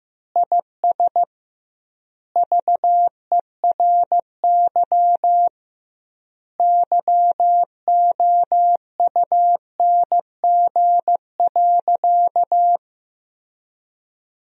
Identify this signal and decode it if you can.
Morse code, heard as a keyed tone, about 15 wpm, 700 Hz. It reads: IS VERY YOUNG.
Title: 2x Word-Spacing - Sentences from Top 400 Words - 15wpm